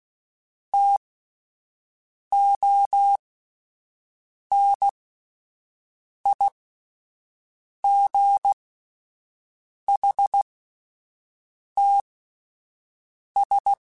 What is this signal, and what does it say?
Morse code, heard as a keyed tone, about 16 wpm, spaced as for 5 wpm, 800 Hz. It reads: TONIGHTS
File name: Spaced Characters / Spaced Characters-mp3s / Weather 5 1 sp @10